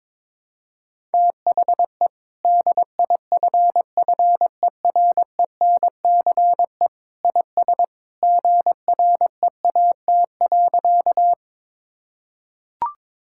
Morse code, heard as a keyed tone, about 22 wpm, 700 Hz. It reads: THE DIFFERENCE IS GREAT.
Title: Sentences from Top 1000 Words - Review Part 5 - 22wpm